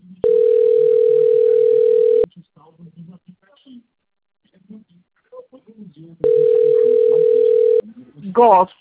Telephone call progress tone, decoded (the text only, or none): ringback tone